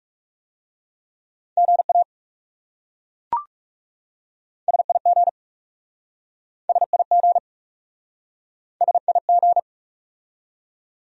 Morse code, heard as a keyed tone, about 40 wpm, 700 Hz. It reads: GA SIG SIG SIG